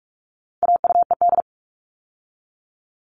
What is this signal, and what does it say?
Morse code, heard as a keyed tone, about 44 wpm, 700 Hz. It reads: AUED
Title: All Cut Numbers - 4-Digit 45wpm